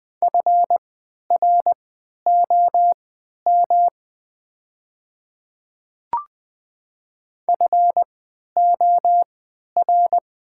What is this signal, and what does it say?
Morse code, heard as a keyed tone, about 20 wpm, spaced as for 11 wpm, 690 Hz. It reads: FROM FOR
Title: 3x Character Spacing - Top 200 Words - 20wpm